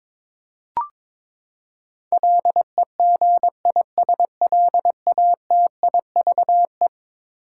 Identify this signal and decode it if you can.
Morse code, heard as a keyed tone, about 22 wpm, 700 Hz. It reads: LEGISLATIVE
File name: practice / 22wpm / Eleven Letter Words - 22wpm